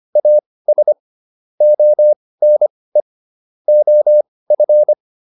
Morse code, heard as a keyed tone, about 25 wpm, spaced as for 18 wpm, 600 Hz. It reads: AS ONE OF